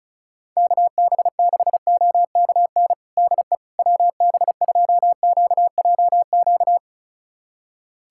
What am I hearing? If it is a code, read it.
Morse code, heard as a keyed tone, about 36 wpm, 700 Hz. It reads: KB6OKN DE WB2QJQ
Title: USA Callsign DE Callsign - 6 Characters - 35wpm